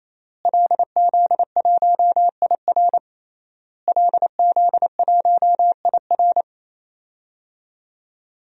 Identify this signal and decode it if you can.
Morse code, heard as a keyed tone, about 28 wpm, 700 Hz. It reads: LZ1IR LZ1IR
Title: DXCC-100 Call Signs - 5 Characters - Repeated 3x - 28wpm